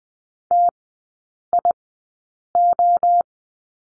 Morse code, heard as a keyed tone, about 20 wpm, spaced as for 8 wpm, 700 Hz. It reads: TIO
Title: 047 Lesson 9e - Number 4 - Three Characters 20wpm